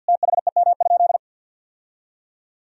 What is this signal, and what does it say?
Morse code, heard as a keyed tone, about 50 wpm, 700 Hz. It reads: THEM?
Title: Sentences from Top 1000 Words - Review Part 4 - 50wpm